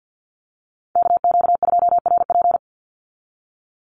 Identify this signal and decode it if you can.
Morse code, heard as a keyed tone, about 49 wpm, 700 Hz. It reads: KQ2RP
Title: NA SOTA Chaser Callsigns 50wpm